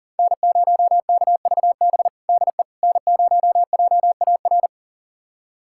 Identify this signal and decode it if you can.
Morse code, heard as a keyed tone, about 40 wpm, 700 Hz. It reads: N0KVB DE N0JAR
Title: USA Callsign DE Callsign - 5 Characters - 40wpm